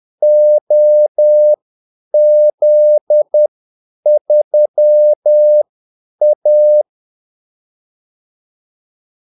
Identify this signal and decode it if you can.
Morse code, heard as a keyed tone, about 10 wpm, spaced as for 8 wpm, 600 Hz. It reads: OZ3A
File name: Callsigns-10-8wpm